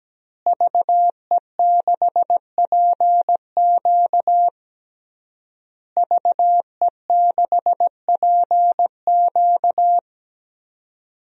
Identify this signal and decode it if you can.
Morse code, heard as a keyed tone, about 17 wpm, 700 Hz. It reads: VE6PQ VE6PQ
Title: DXCC-100 Call Signs - 5 Characters - Repeated 3x - 17wpm